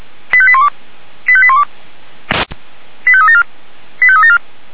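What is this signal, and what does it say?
Five-tone selective call (ZVEI), heard as quick runs of five tones, tones of about 70 ms. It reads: 85812 85812 85374 85374
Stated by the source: ZVEI five-tone selective call 89247 received by SDR, Feuerwehr Orlamünde fire brigade.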